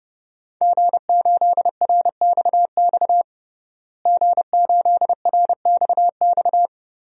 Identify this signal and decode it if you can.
Morse code, heard as a keyed tone, about 30 wpm, 700 Hz. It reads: G8RXX G8RXX
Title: DXCC-100 Call Signs - 5 Characters - Repeated 2x - 30wpm